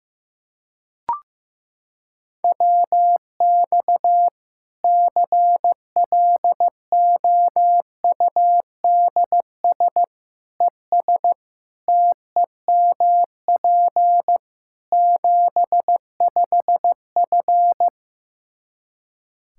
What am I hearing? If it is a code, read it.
Morse code, heard as a keyed tone, about 15 wpm, 700 Hz. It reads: WX CLOUDS ES TEMP 75F
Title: Practice QSO - Exchange 2 - Info - 15wpm